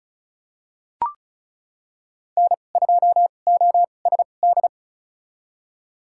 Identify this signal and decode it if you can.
Morse code, heard as a keyed tone, about 34 wpm, spaced as for 25 wpm, 700 Hz. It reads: N2OSD
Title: USA Call Signs - 5 Characters - 2x Character Spacing - 35wpm